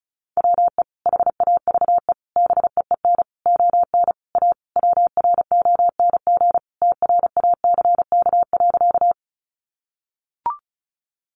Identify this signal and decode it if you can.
Morse code, heard as a keyed tone, about 34 wpm, 700 Hz. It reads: WE HAVE BEEN ON A WRONG TRACK.
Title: Sentences from Top 1100 Words 35wpm